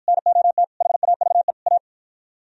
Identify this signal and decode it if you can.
Morse code, heard as a keyed tone, about 53 wpm, 700 Hz. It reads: NOT HAVE A